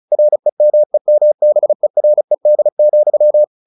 Morse code, heard as a keyed tone, about 34 wpm, 600 Hz. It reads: REMEMBERED,
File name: the old man and the sea / the old man and the sea 1 35wpm